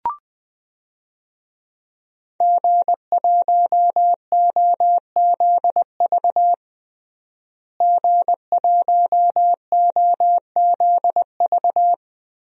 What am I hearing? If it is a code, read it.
Morse code, heard as a keyed tone, about 21 wpm, 700 Hz. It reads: G1OZV G1OZV
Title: DXCC-100 Call Signs - 5 Characters - Repeated 2x - 20wpm